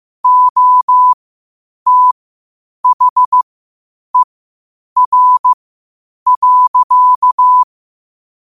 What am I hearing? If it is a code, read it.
Morse code, heard as a keyed tone, about 15 wpm, spaced as for 8 wpm, 1000 Hz. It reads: OTHER.
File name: qotdfiles / 8wpm